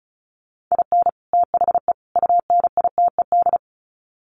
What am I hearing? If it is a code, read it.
Morse code, heard as a keyed tone, about 34 wpm, 700 Hz. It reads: IN THE UNITED
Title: Sets of 3 Words 35wpm